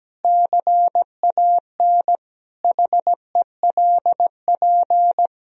Morse code, heard as a keyed tone, about 17 wpm, 700 Hz. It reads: CAN HELP
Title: Sentences from Top 1000 Words - Review Part 3 - 17wpm